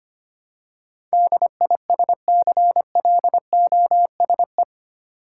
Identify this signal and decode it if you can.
Morse code, heard as a keyed tone, about 25 wpm, 700 Hz. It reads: DISCLOSE
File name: 8 Letter Words - First 3 Letters Given - 25wpm